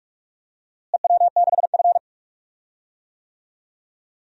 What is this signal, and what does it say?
Morse code, heard as a keyed tone, about 45 wpm, 700 Hz. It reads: EW6F
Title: DXCC-100 Call Signs - 4 Characters - Repeated 3x - 45wpm